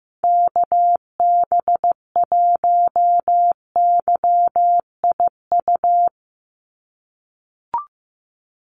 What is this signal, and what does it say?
Morse code, heard as a keyed tone, about 15 wpm, 700 Hz. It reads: KB1YIU
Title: USA Call Signs 15wpm